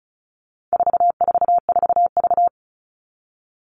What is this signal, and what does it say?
Morse code, heard as a keyed tone, about 34 wpm, 700 Hz. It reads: 444V
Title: All Cut Numbers - 4-Digit 35wpm